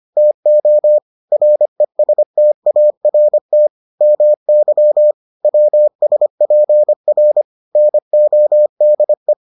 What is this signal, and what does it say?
Morse code, heard as a keyed tone, about 25 wpm, 600 Hz. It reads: TO RESTART MY WSPR NODE